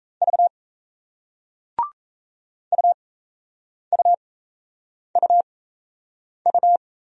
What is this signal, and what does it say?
Morse code, heard as a keyed tone, about 38 wpm, spaced as for 8 wpm, 700 Hz. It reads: V UUUU